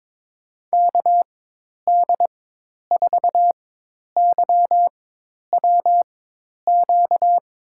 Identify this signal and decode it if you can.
Morse code, heard as a keyed tone, about 23 wpm, spaced as for 10 wpm, 700 Hz. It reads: KD4YWQ
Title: USA Call Signs - 6 Characters - 4x Character Spacing - 22wpm